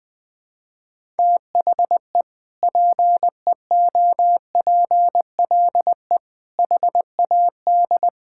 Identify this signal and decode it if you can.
Morse code, heard as a keyed tone, about 20 wpm, 700 Hz. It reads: THE PEOPLE HAD
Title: Sentences from Top 1000 Words - Review Part 6 - 20wpm